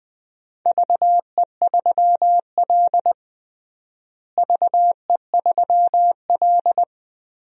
Morse code, heard as a keyed tone, about 21 wpm, 700 Hz. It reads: VE3L VE3L
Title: DXCC-100 Call Signs - 4 Characters - Repeated 2x - 20wpm